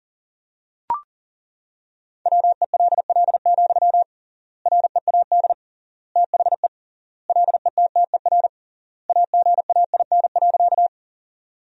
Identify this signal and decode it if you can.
Morse code, heard as a keyed tone, about 40 wpm, 700 Hz. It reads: WELL, READ THE LETTER AGAIN.
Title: 3x Word-Spacing - Sentences from Top 300 Words - 40wpm